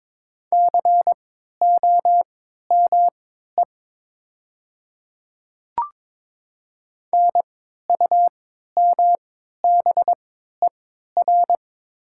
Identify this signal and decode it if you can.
Morse code, heard as a keyed tone, about 22 wpm, spaced as for 13 wpm, 700 Hz. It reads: COME NUMBER
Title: 3x Character Spacing - Top 100 Words - 22wpm